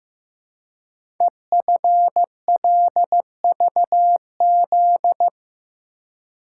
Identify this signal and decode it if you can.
Morse code, heard as a keyed tone, about 15 wpm, 700 Hz. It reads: EFLVZ
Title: Code Groups - 5 Letters - 15wpm